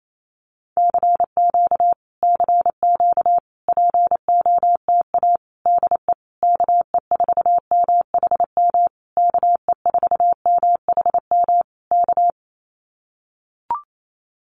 Morse code, heard as a keyed tone, about 28 wpm, 700 Hz. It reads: CQ CQ POTA DE KE4MHM KE4MHM K
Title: Common QSO Phrases Encore 28wpm